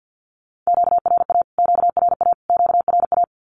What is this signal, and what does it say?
Morse code, heard as a keyed tone, about 50 wpm, 700 Hz. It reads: QRA QRA QRA
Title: Q-Codes Repeated 3x 50wpm